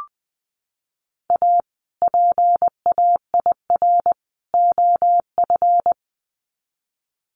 Morse code, heard as a keyed tone, about 20 wpm, 700 Hz. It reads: A PAIR OF
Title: Sets of 3 Words 20wpm